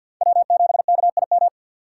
Morse code, heard as a keyed tone, about 48 wpm, 700 Hz. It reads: W7KIM
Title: USA Call Signs - 5 Characters - 6x Character Spacing - 50wpm